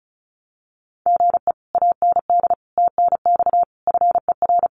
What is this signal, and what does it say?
Morse code, heard as a keyed tone, about 36 wpm, 700 Hz. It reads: GE AND TNX FER
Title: Common QSO Phrases Encore 35wpm